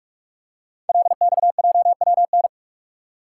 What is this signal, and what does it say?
Morse code, heard as a keyed tone, about 46 wpm, 700 Hz. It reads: LXJWN
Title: Code Groups - 5 Letter:Numbers - 45wpm